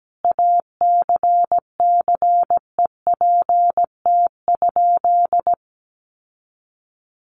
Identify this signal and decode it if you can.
Morse code, heard as a keyed tone, about 17 wpm, 700 Hz. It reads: ACCEPT?
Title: Sentences from Top 1300 Words 17wpm